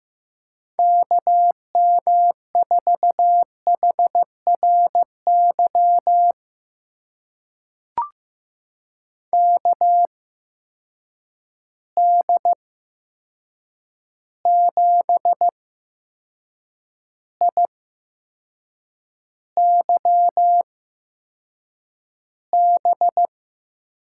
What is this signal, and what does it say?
Morse code, heard as a keyed tone, about 15 wpm, spaced as for 6 wpm, 700 Hz. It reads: KM4HRY KD7IYB
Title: USA Call Signs - 6 Characters - 8x Character Spacing - 15wpm